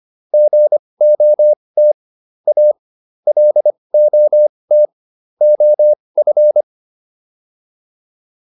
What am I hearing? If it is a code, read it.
Morse code, heard as a keyed tone, about 25 wpm, spaced as for 20 wpm, 600 Hz. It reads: GOT A LOT OF